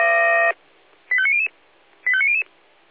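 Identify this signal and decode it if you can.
Five-tone selective call (ZVEI), heard as quick runs of five tones, tones of about 70 ms. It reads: 85900 85900
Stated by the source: ZVEI five-tone selective call received on an SDR.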